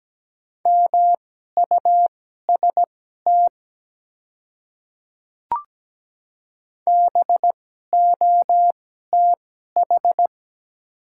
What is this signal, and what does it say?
Morse code, heard as a keyed tone, about 17 wpm, spaced as for 12 wpm, 690 Hz. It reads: MUST BOTH
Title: 2x Character Spacing - Top 300 Words - 17wpm